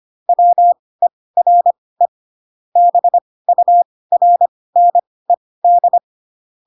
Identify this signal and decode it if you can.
Morse code, heard as a keyed tone, about 25 wpm, spaced as for 18 wpm, 700 Hz. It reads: WERE BURNED